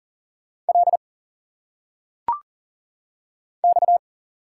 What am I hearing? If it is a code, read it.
Morse code, heard as a keyed tone, about 40 wpm, 700 Hz. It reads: L X
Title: Single Letters - Phonetic Pronunciation 40wpm